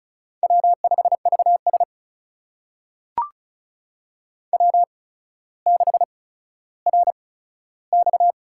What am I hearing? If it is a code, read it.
Morse code, heard as a keyed tone, about 35 wpm, spaced as for 15 wpm, 700 Hz. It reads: W5VS W6RX
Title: USA Call Signs - 4 Characters - 8x Character Spacing - 35wpm